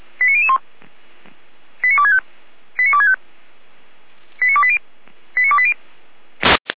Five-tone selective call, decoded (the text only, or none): 89001 89265 89265 89280 89280